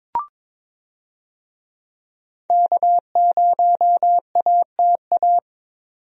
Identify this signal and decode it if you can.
Morse code, heard as a keyed tone, about 22 wpm, 700 Hz. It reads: K0ATA
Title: USA Call Signs - 5 Characters - 22wpm